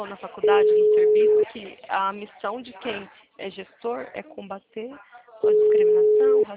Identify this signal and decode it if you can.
Ringback tone (Europe, 425 Hz), with speech.